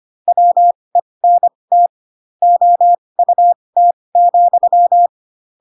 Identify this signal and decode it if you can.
Morse code, heard as a keyed tone, about 25 wpm, spaced as for 20 wpm, 700 Hz. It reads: WENT OUT,
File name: Pinocchio / Pinocchio ch7 15 25 ik1hge